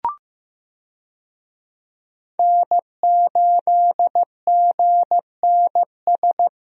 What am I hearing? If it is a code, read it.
Morse code, heard as a keyed tone, about 15 wpm, 700 Hz. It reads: N8GNS
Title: USA Call Signs - 5 Characters - Repeated 2x - 15wpm